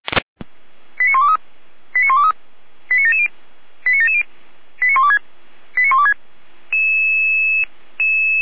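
Five-tone selective call (ZVEI), heard as quick runs of five tones, tones of about 70 ms. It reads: 89123 89123 89770 89770 89126 89126
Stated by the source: ZVEI five-tone selective call received by SDR.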